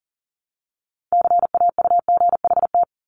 Morse code, heard as a keyed tone, about 39 wpm, 700 Hz. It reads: CAUGHT
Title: Multi-letter Phonemes Repeated 3x 40wpm